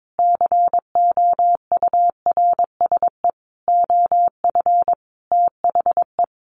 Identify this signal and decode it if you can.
Morse code, heard as a keyed tone, about 22 wpm, 700 Hz. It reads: COURSE OF THE
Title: Sets of 5 Words Encore 22wpm